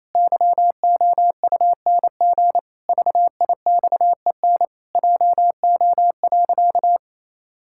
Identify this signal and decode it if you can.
Morse code, heard as a keyed tone, about 27 wpm, 700 Hz. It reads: YOUNG VIXEN JO.
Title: Pangram Challenge v2 - 28wpm